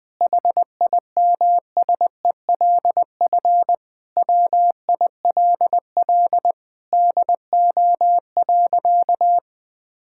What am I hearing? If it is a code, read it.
Morse code, heard as a keyed tone, about 20 wpm, 700 Hz. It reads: HIMSELF WILL DO.
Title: Sentences from Top 1600 Words 20wpm